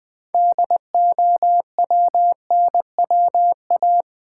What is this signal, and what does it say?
Morse code, heard as a keyed tone, about 20 wpm, 700 Hz. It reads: DOWNWA
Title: Build A Word - 3 Letters At A Time - 9-Letter Words - 20wpm